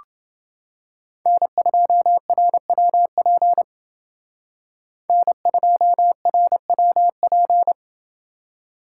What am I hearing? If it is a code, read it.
Morse code, heard as a keyed tone, about 28 wpm, 700 Hz. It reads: N2RWP N2RWP